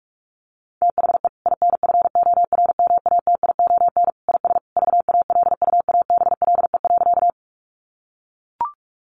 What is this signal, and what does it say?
Morse code, heard as a keyed tone, about 44 wpm, 700 Hz. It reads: THE INFORMATION IS VALUABLE.
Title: Sentences from Top 1100 Words 45wpm